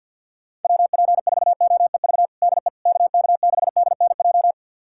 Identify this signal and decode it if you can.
Morse code, heard as a keyed tone, about 52 wpm, 700 Hz. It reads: WP4OEV DE KK6DNJ
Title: USA Callsign DE Callsign - 6 Characters - 50wpm